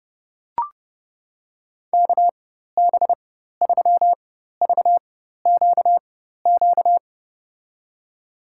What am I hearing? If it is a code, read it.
Morse code, heard as a keyed tone, about 32 wpm, spaced as for 14 wpm, 700 Hz. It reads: KB3VQQ
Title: USA Call Signs - 4x Character Spacing - 30wpm